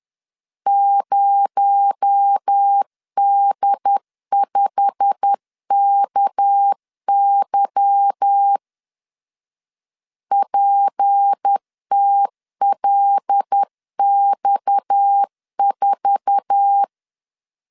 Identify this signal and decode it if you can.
Morse code, heard as a keyed tone, about 10 wpm, 800 Hz. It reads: 0D5KY PTLX4
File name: LEZIONI / 8 YJ45 / 8 YJ45 10wpm 2549